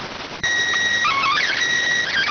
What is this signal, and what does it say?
SSTV (slow-scan television), PD 290.
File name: findings / CT 205 ax mips, iDose 3 0006.gen-28 rx